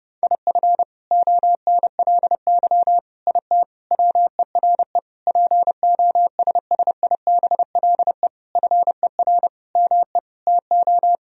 Morse code, heard as a keyed tone, about 32 wpm, 700 Hz. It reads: IF ONLY IT WERE POSSIBLE FER ME TO
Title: Sentences from Top 500 Words and Abbreviations 30wpm